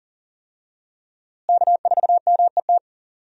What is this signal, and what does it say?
Morse code, heard as a keyed tone, about 40 wpm, 700 Hz. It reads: K4MET
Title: USA Call Signs - 5 Characters - Repeated 2x - 40wpm